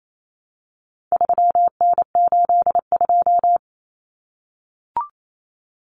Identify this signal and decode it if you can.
Morse code, heard as a keyed tone, about 28 wpm, 700 Hz. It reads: 3N82
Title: Common Cut Numbers - 4-Digit 28wpm